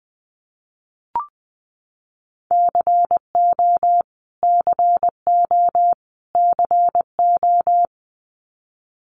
Morse code, heard as a keyed tone, about 20 wpm, 700 Hz. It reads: CO CO CO
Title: US States Repeated 3x 20wpm